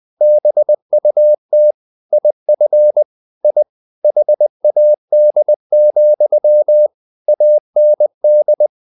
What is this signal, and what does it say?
Morse code, heard as a keyed tone, about 20 wpm, 600 Hz. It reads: BUT IF I HAD, AND
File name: the old man and the sea part 2 20wpm